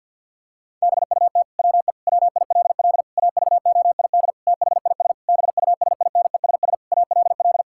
Morse code, heard as a keyed tone, about 51 wpm, 700 Hz. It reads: BUT WE WILL AVOID THIS BUSINESS ALL
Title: Sentences from Top 1000 Words - Review Part 6 - 50wpm